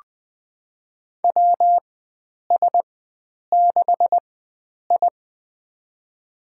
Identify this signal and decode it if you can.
Morse code, heard as a keyed tone, about 20 wpm, spaced as for 9 wpm, 700 Hz. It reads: WS6I
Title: USA Call Signs - 4 Characters - 4x Character Spacing - 20wpm